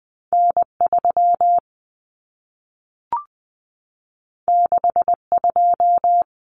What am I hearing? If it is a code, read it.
Morse code, heard as a keyed tone, about 20 wpm, 700 Hz. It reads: N 3 6 2
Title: Common Cut Numbers - 2-Digit 20wpm